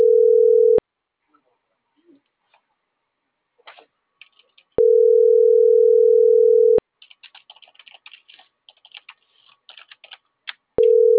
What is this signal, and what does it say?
Ringback tone (North America), 440 + 480 Hz.
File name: CENTRAL NAVIERAS / out juan.salcedo